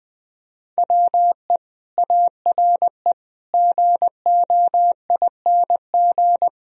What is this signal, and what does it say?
Morse code, heard as a keyed tone, about 23 wpm, 700 Hz. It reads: WE ARE GOING
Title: Speed-Racing - Sets of 4 Words 20wpm